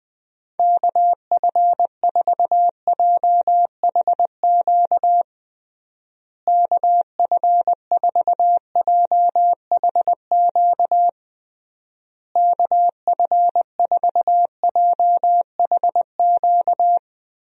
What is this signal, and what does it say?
Morse code, heard as a keyed tone, about 20 wpm, 700 Hz. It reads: KF4JHQ KF4JHQ KF4JHQ
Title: USA Call Signs - 6 Characters - Repeated 3x - 20wpm